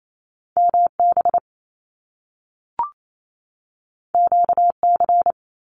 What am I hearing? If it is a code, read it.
Morse code, heard as a keyed tone, about 28 wpm, 700 Hz. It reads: MB QC
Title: Canadian Provinces and Territories 28wpm